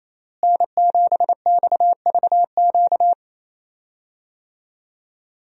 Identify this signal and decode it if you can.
Morse code, heard as a keyed tone, about 28 wpm, 700 Hz. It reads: N7XVQ
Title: USA Call Signs - 5 Characters - Repeated 2x - 28wpm